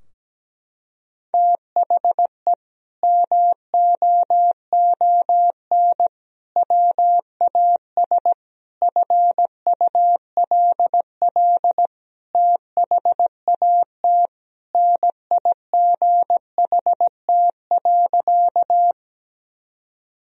Morse code, heard as a keyed tone, about 17 wpm, 700 Hz. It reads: THE MOON WAS FULL THAT NIGHT.
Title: Sentences from Top 1000 Words - Review Part 3 - 17wpm